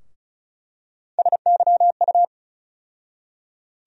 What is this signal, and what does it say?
Morse code, heard as a keyed tone, about 34 wpm, 700 Hz. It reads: SYU
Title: USA Call Sign Postfixes - 3 Letters 35wpm